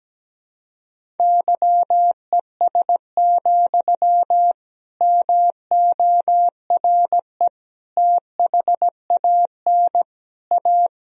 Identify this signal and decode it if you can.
Morse code, heard as a keyed tone, about 17 wpm, 700 Hz. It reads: YES, MORE THAN A
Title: Sentences from Top 1000 Words - Review Part 1 - 17wpm